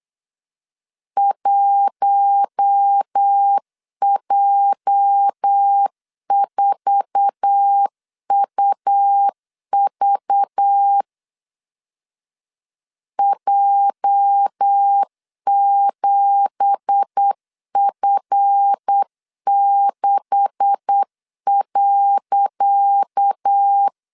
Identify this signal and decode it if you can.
Morse code, heard as a keyed tone, about 8 wpm, 800 Hz. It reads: 1J4UV J7F6.
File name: LEZIONI / 10 SIGNS / 10D COMA 8wpm 3749